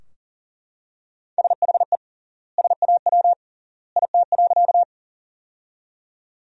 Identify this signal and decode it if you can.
Morse code, heard as a keyed tone, about 40 wpm, 700 Hz. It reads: SHE SAW IT.
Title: 3x Word-Spacing - Sentences from Top 300 Words - 40wpm